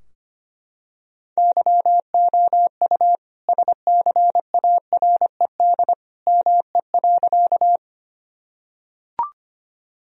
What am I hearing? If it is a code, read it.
Morse code, heard as a keyed tone, about 25 wpm, 700 Hz. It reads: YOU SCARED ME.
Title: Sentences from Top 1000 Words - Review Part 4 - 25wpm